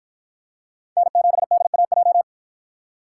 Morse code, heard as a keyed tone, about 53 wpm, 700 Hz. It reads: N7DAJ